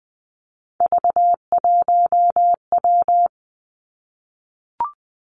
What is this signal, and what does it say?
Morse code, heard as a keyed tone, about 20 wpm, 700 Hz. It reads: V1W
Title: Three Challenging Letter-Number Combinations 20wpm